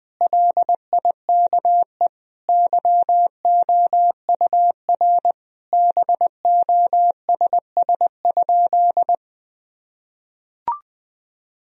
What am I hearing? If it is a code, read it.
Morse code, heard as a keyed tone, about 21 wpm, 700 Hz. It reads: LIKE YOUR BOSS?
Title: Sentences from Top 1000 Words - Review Part 4 - 20wpm